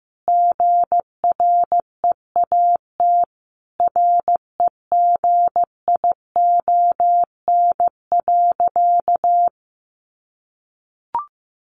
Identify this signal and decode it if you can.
Morse code, heard as a keyed tone, about 15 wpm, 700 Hz. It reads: GREAT REGION.
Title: Sentences from Top 1500 Words 15wpm